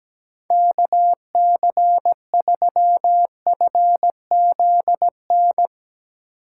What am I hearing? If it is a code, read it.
Morse code, heard as a keyed tone, about 18 wpm, 700 Hz. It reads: KC3FZN
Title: Speed Racing - USA 2x3 Call Signs 17wpm